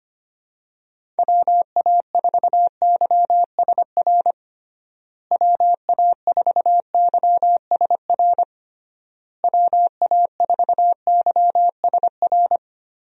Morse code, heard as a keyed tone, about 25 wpm, 700 Hz. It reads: WA4YSR WA4YSR WA4YSR
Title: USA Call Signs - 6 Characters - Repeated 3x - 25wpm